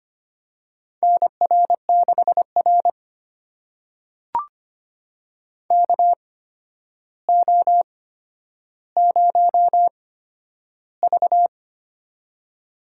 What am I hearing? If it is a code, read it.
Morse code, heard as a keyed tone, about 25 wpm, spaced as for 11 wpm, 700 Hz. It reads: NR6R KO0V
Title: USA Call Signs - 4 Characters - 8x Character Spacing - 25wpm